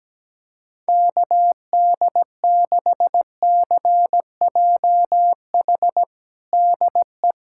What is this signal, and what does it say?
Morse code, heard as a keyed tone, about 17 wpm, 700 Hz. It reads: KD6CJH DE
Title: USA Callsign DE Callsign 17wpm